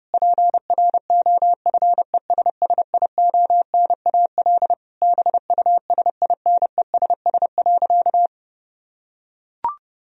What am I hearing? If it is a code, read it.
Morse code, heard as a keyed tone, about 30 wpm, 700 Hz. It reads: PROFESSIONAL BUSINESS.
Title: Sentences from Top 1000 Words - Review Part 4 - 30wpm